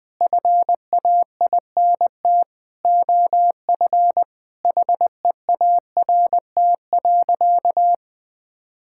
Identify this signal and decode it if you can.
Morse code, heard as a keyed tone, about 20 wpm, 700 Hz. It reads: FAINT OF HEART.
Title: Sentences with 7-Word Phrases - Spoken Beginning - 20wpm